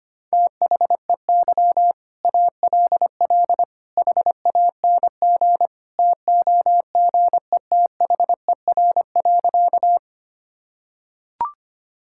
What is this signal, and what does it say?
Morse code, heard as a keyed tone, about 25 wpm, 700 Hz. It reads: THEY ALL HANG TOGETHER.
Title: Sentences from Top 1600 Words 25wpm